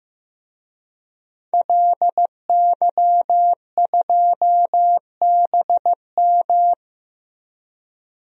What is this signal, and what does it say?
Morse code, heard as a keyed tone, about 15 wpm, 700 Hz. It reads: LY2BM
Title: DXCC-100 Call Signs - 5 Characters - 15wpm